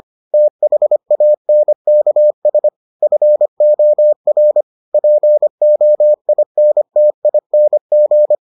Morse code, heard as a keyed tone, about 25 wpm, 600 Hz. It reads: THANKS FOR POINTING